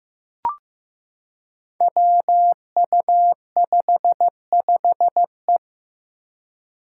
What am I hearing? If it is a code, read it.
Morse code, heard as a keyed tone, about 15 wpm, 700 Hz. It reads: WU55E
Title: Five Challenging Letter-Number Combinations 15wpm